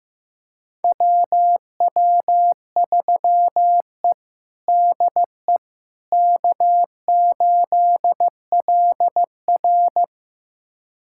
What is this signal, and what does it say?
Morse code, heard as a keyed tone, about 15 wpm, 700 Hz. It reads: WW3E DE K8LR
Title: USA Callsign DE Callsign - 4 Characters - 15wpm